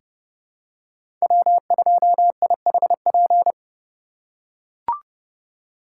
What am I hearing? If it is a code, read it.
Morse code, heard as a keyed tone, about 32 wpm, 700 Hz. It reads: W2IHP
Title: USA Call Signs - 2x Character Spacing - 30wpm